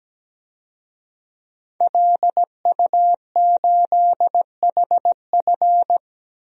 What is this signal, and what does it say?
Morse code, heard as a keyed tone, about 17 wpm, 700 Hz. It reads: LU8HF
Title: DXCC-100 Call Signs - 5 Characters - 17wpm